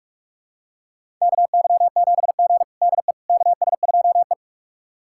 Morse code, heard as a keyed tone, about 46 wpm, 700 Hz. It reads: KY7G DE KS2E
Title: USA Callsign DE Callsign - 4 Characters - 45wpm